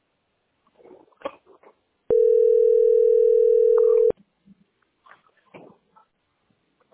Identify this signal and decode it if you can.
Ringback tone (North America), 440 + 480 Hz.